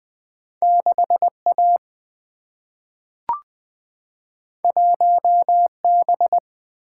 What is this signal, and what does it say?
Morse code, heard as a keyed tone, about 20 wpm, 700 Hz. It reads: 6A 1B
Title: Single-Letter Single-Number 20wpm